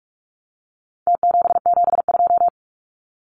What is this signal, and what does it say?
Morse code, heard as a keyed tone, about 44 wpm, 700 Hz. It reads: T772
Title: Numbers - Contest Style - 4-Digit 45wpm